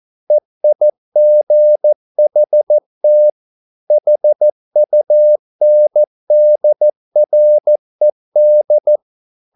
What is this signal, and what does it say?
Morse code, heard as a keyed tone, about 14 wpm, 600 Hz. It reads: EIGHT HUNDRED